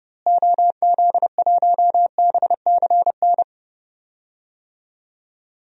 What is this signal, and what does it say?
Morse code, heard as a keyed tone, about 30 wpm, 700 Hz. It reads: OZ1BCN
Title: DXCC-100 Call Signs - 6 Characters - Repeated 3x - 30wpm